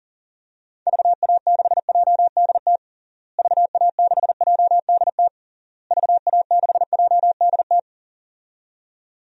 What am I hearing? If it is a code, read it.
Morse code, heard as a keyed tone, about 40 wpm, 700 Hz. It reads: VA6JDT VA6JDT VA6JDT
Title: DXCC-100 Call Signs - 6 Characters - Repeated 3x - 40wpm